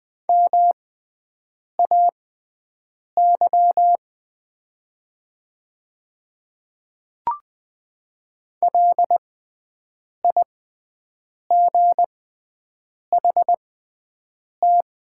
Morse code, heard as a keyed tone, about 20 wpm, spaced as for 7 wpm, 700 Hz. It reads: MAY LIGHT
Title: 6x Character Spacing - Top 500 Words - 20wpm